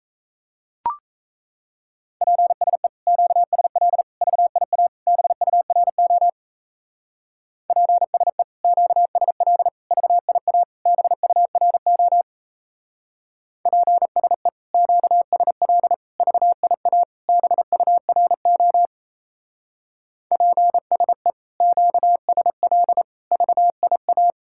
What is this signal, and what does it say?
Morse code, heard as a keyed tone, about 36 wpm, 700 Hz. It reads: PSE QSL VIA BURO PSE QSL VIA BURO PSE QSL VIA BURO PSE QSL VIA